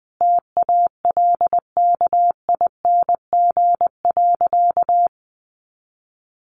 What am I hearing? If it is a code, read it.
Morse code, heard as a keyed tone, about 20 wpm, 700 Hz. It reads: TALKING.
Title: Additional Easy Phrases 20wpm